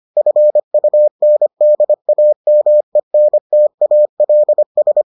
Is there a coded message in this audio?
Yes: Morse code from 0.2 to 5.0 s, heard as a keyed tone, about 25 wpm, 600 Hz. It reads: FUNDAMENTALS